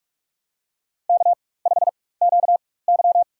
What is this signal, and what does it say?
Morse code, heard as a keyed tone, about 46 wpm, spaced as for 21 wpm, 700 Hz. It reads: K5QY
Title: USA Call Signs - 4 Characters - 4x Character Spacing - 45wpm